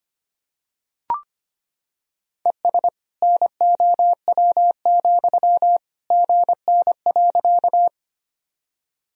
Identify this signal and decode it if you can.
Morse code, heard as a keyed tone, about 25 wpm, 700 Hz. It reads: ES NOW, GN.
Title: Sentences from Top 300 Words and Abbreviations 25wpm